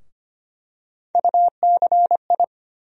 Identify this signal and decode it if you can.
Morse code, heard as a keyed tone, about 25 wpm, 700 Hz. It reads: UCI
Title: USA Call Sign Postfixes - 3 Letters 25wpm